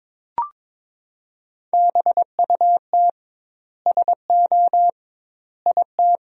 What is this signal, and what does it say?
Morse code, heard as a keyed tone, about 23 wpm, 700 Hz. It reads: BUT SO IT
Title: 2x Word-Spacing - Sentences from Top 100 Words - 22wpm